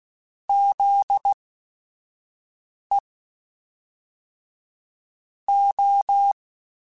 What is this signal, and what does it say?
Morse code, heard as a keyed tone, about 16 wpm, 800 Hz. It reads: Z E O
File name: Technical Subjects / Technical Subjects-mp3s / Sw Pwr Sup @5